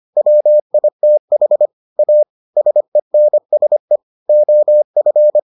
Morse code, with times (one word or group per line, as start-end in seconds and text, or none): 0.2-1.7 WITH
2.0-2.2 A
2.6-4.0 SENSE
4.3-5.4 OF